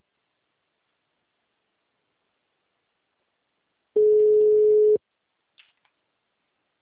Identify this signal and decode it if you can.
Ringback tone (Europe, 425 Hz).